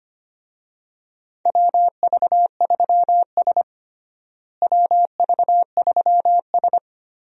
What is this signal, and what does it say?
Morse code, heard as a keyed tone, about 25 wpm, 700 Hz. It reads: WV3S WV3S
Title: USA Call Signs - 4 Characters - Repeated 3x - 25wpm